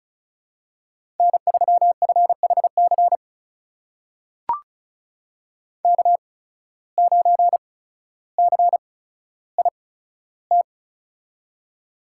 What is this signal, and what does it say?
Morse code, heard as a keyed tone, about 36 wpm, spaced as for 15 wpm, 700 Hz. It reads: N3FHC K9CIT
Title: USA Call Signs - 5 Characters - 8x Character Spacing - 35wpm